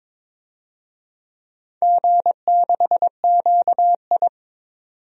Morse code, heard as a keyed tone, about 22 wpm, 700 Hz. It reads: G6QI
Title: DXCC-100 Call Signs - 4 Characters - 22wpm